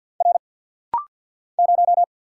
Morse code, heard as a keyed tone, about 50 wpm, 700 Hz. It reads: R 0
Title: Single Letter-Number - Rapid-Fire 50wpm